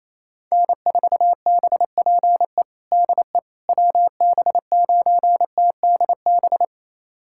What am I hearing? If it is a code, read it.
Morse code, heard as a keyed tone, about 27 wpm, 700 Hz. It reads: N4BPE DE WB9TDB
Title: USA Callsign DE Callsign 28wpm